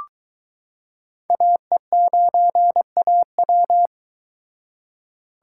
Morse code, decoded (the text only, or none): AE9AW